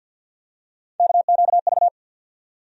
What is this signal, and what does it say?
Morse code, heard as a keyed tone, about 52 wpm, 700 Hz. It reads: KQV
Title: USA Call Sign Postfixes - 3 Letters 50wpm